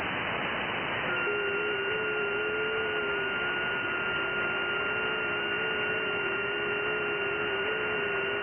FT8.
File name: IC-7100